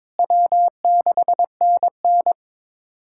Morse code, heard as a keyed tone, about 21 wpm, 700 Hz. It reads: W6NN
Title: USA Callsign DE Callsign - 4 Characters - 22wpm